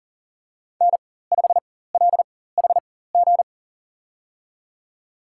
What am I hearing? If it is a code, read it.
Morse code, heard as a keyed tone, about 37 wpm, spaced as for 18 wpm, 700 Hz. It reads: N5LHG
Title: USA Call Signs - 5 Characters - 4x Character Spacing - 40wpm